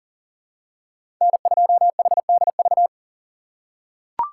Morse code, heard as a keyed tone, about 40 wpm, 700 Hz. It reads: N2HDV
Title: USA Call Signs - 4x Character Spacing - 40wpm